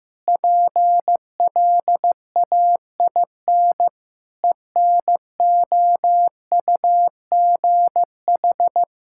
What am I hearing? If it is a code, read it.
Morse code, heard as a keyed tone, about 15 wpm, 700 Hz. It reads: PLAIN ENOUGH
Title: Sentences from Top 1000 Words - Review Part 5 - 15wpm